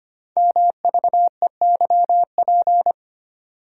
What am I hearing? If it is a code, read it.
Morse code, heard as a keyed tone, about 25 wpm, 700 Hz. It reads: MVEYP
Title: Code Groups - 5 Letters - 25wpm